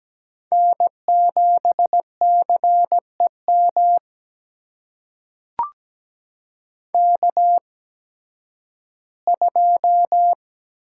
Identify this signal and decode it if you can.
Morse code, heard as a keyed tone, about 17 wpm, spaced as for 11 wpm, 700 Hz. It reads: N7CEM K2
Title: USA Call Signs - 5 Characters - 8x Character Spacing - 17wpm